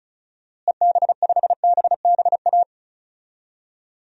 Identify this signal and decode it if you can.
Morse code, heard as a keyed tone, about 36 wpm, 700 Hz. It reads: EB5BBA
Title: DXCC-100 Call Signs - 6 Characters - Repeated 3x - 35wpm